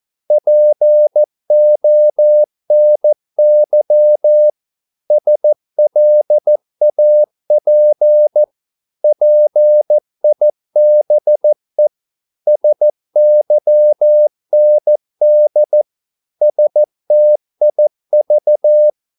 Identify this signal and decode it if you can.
Morse code, heard as a keyed tone, about 14 wpm, 600 Hz. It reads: PONY SLAP PIBE SYND STIV
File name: Korte ord DK 14wpm